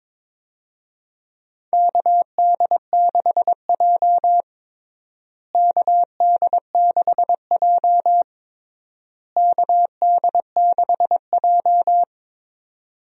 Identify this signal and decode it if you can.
Morse code, heard as a keyed tone, about 22 wpm, 700 Hz. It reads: KD6J KD6J KD6J
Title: USA Call Signs - 4 Characters - Repeated 3x - 22wpm